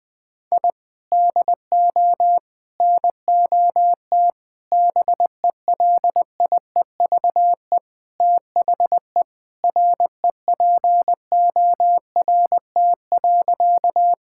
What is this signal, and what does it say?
Morse code, heard as a keyed tone, about 20 wpm, 700 Hz. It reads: I DO NOT BELIEVE THE REPORT.
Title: Sentences from Top 1000 Words - Review Part 6 - 20wpm